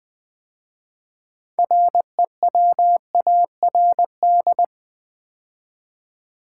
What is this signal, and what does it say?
Morse code, heard as a keyed tone, about 20 wpm, 690 Hz. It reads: REWARD
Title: 6 Letter Words - First 3 Letters Given - 20wpm